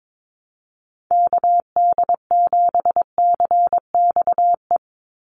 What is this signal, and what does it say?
Morse code, heard as a keyed tone, about 22 wpm, 700 Hz. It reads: KD7CXE
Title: USA Call Signs 22wpm